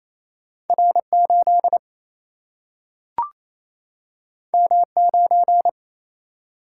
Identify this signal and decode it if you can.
Morse code, heard as a keyed tone, about 28 wpm, 700 Hz. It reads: R8 M9
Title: Single-Letter Single-Number 28wpm